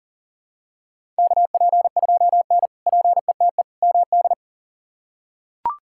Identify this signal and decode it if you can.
Morse code, heard as a keyed tone, about 40 wpm, 700 Hz. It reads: KP2N PETE MD
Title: K1USN SST QSO Practice - 40wpm